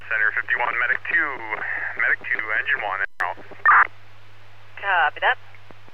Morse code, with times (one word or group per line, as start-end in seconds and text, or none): none